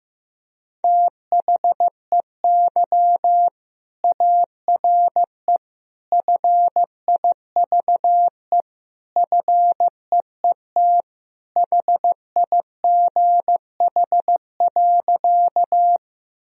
Morse code, heard as a keyed tone, about 15 wpm, 700 Hz. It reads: THEY ARE FIVE FEET HIGH.
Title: Sentences from Top 1000 Words - Review Part 2 - 15wpm